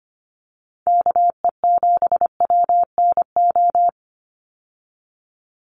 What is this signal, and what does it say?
Morse code, heard as a keyed tone, about 25 wpm, 700 Hz. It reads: KE7WNO
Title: USA Call Signs 25wpm